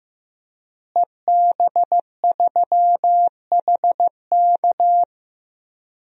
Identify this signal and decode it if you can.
Morse code, heard as a keyed tone, about 15 wpm, 700 Hz. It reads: EB3HK
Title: DXCC-100 Call Signs - 5 Characters - Repeated 2x - 15wpm